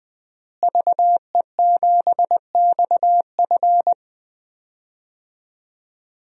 Morse code, heard as a keyed tone, about 20 wpm, 700 Hz. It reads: VE7XF
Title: DXCC-100 Call Signs - 5 Characters - Repeated 2x - 20wpm